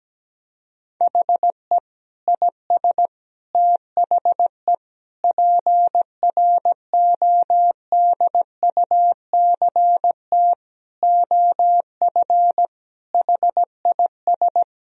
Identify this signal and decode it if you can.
Morse code, heard as a keyed tone, about 17 wpm, 700 Hz. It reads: HE IS THE PRODUCT OF HIS
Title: Sentences from Top 1000 Words - Review Part 2 - 17wpm